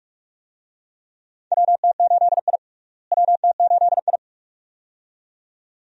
Morse code, heard as a keyed tone, about 47 wpm, 700 Hz. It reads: WT8I WT8I
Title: USA Call Signs - 4 Characters - Repeated 2x - 45wpm